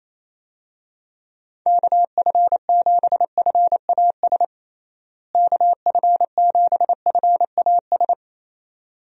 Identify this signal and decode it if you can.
Morse code, heard as a keyed tone, about 27 wpm, 700 Hz. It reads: KF7FAS KF7FAS
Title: USA Call Signs - 6 Characters - Repeated 2x - 28wpm